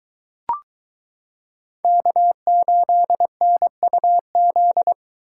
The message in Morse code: K8NUZ

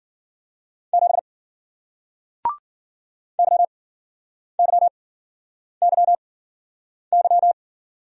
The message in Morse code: 7 YYYY

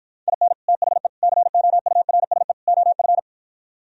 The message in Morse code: IN THE COURSE OF

